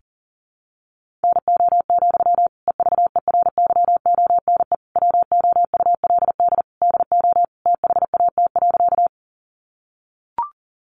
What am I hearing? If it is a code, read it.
Morse code, heard as a keyed tone, about 40 wpm, 700 Hz. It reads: NO, EVERYONE WOULD DO THAT.
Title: Sentences from Top 800 Words 40wpm